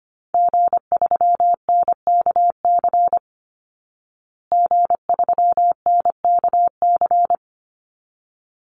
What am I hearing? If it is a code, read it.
Morse code, heard as a keyed tone, about 25 wpm, 700 Hz. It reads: G3NKC G3NKC
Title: Call Signs Repeated 3x - 25wpm